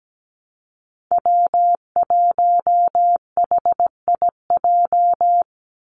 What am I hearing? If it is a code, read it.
Morse code, heard as a keyed tone, about 17 wpm, 700 Hz. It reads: W1HIJ
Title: Call Signs Repeated 2x - 17wpm